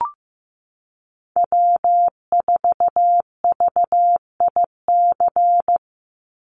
Morse code, heard as a keyed tone, about 14 wpm, 700 Hz. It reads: W4VIC
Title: Call Signs Repeated 3x - 15wpm